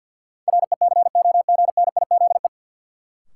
Morse code, heard as a keyed tone, about 52 wpm, 700 Hz. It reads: RECOGNIZE
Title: Sentences from Top 1400 Words - New Word Drill - 50wpm